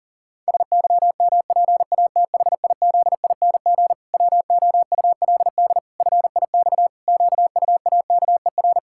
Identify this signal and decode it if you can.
Morse code, heard as a keyed tone, about 40 wpm, 700 Hz. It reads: SYMPATHIZING WOULD FIX QUAKER
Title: Pangram Challenge v2 - 40wpm